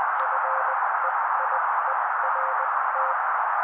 Morse code, heard as a keyed tone, about 20 wpm, spaced as for 16 wpm, 530 Hz. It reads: FEIERT
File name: daily / 16wpm noise